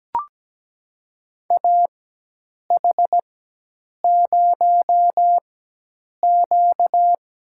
Morse code, heard as a keyed tone, about 17 wpm, spaced as for 8 wpm, 700 Hz. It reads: AH0Q
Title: USA Call Signs - 4x Character Spacing - 17wpm